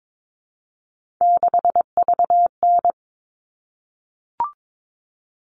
Morse code, heard as a keyed tone, about 22 wpm, 700 Hz. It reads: 6VN
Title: All Cut Numbers - 3-Digit 22wpm